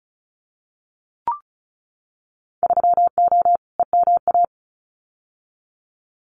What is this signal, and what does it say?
Morse code, heard as a keyed tone, about 36 wpm, 690 Hz. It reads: 3O EMA
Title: Winter Field Day Exchanges 35wpm